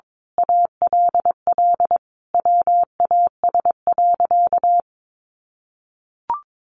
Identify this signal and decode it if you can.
Morse code, heard as a keyed tone, about 22 wpm, 700 Hz. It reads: ALL WAS.
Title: Sentences from Top 1000 Words 22wpm